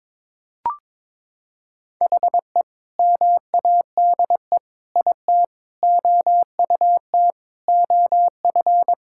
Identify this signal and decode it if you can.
Morse code, heard as a keyed tone, about 22 wpm, 700 Hz. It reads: HE MADE IT OUT OF
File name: practice / 22wpm / Sentences with 3-Word Phrases - 22wpm